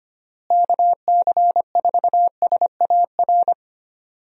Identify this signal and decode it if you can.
Morse code, heard as a keyed tone, about 25 wpm, 700 Hz. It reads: KC4SAR
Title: USA Call Signs - 6 Characters - Repeated 2x - 25wpm